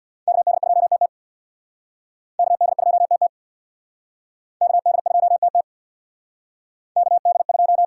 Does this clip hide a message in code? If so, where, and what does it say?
Yes: Morse code from 0.3 to 7.8 s, heard as a keyed tone, about 66 wpm, 700 Hz. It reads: KD2TT KD2TT KD2TT KD2